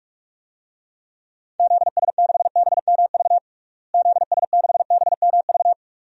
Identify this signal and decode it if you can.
Morse code, heard as a keyed tone, about 46 wpm, 700 Hz. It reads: ZS6BMV ZS6BMV
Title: DXCC-100 Call Signs - 6 Characters - Repeated 3x - 45wpm